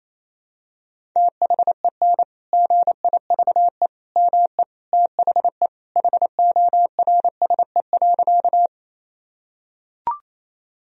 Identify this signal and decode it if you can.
Morse code, heard as a keyed tone, about 28 wpm, 700 Hz. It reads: THEN GIVE ME THE HORSE.
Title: Sentences from Top 1000 Words - Review Part 1 - 28wpm